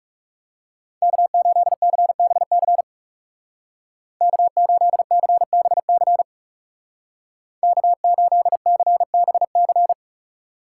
Morse code, heard as a keyed tone, about 37 wpm, 700 Hz. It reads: K8CBC K8CBC K8CBC
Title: Speed Racing - USA Call Signs 30wpm